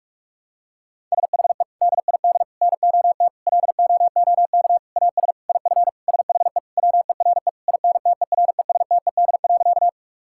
Morse code, heard as a keyed tone, about 46 wpm, 700 Hz. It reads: SHE DID NOT LOOK AS IF SHE WERE INTERESTED.
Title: Sentences from Top 1000 Words - Review Part 6 - 45wpm